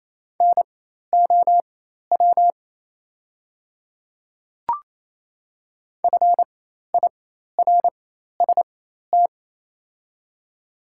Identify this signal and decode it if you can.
Morse code, heard as a keyed tone, about 28 wpm, spaced as for 13 wpm, 700 Hz. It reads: NOW FIRST
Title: 4x Character Spacing - Top 100 Words - 28wpm